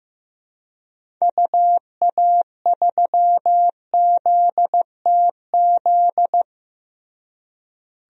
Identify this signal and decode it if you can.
Morse code, heard as a keyed tone, about 15 wpm, 700 Hz. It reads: UA3ZTZ
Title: DXCC-100 Call Signs - 6 Characters - Repeated 2x - 15wpm